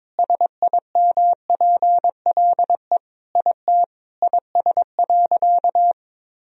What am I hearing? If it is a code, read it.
Morse code, heard as a keyed tone, about 22 wpm, 700 Hz. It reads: SIMPLE IT IS.
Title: Sentences from Top 400 Words and Abbreviations 22wpm